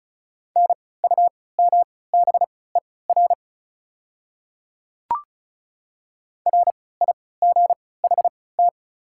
Morse code, heard as a keyed tone, about 34 wpm, spaced as for 20 wpm, 690 Hz. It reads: NUMBER RIGHT
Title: 3x Character Spacing - Top 500 Words - 35wpm